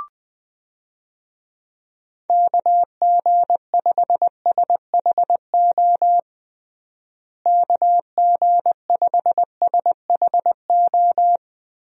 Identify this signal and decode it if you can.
Morse code, heard as a keyed tone, about 20 wpm, 700 Hz. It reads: KG5SHO KG5SHO
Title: USA Call Signs - 6 Characters - Repeated 2x - 20wpm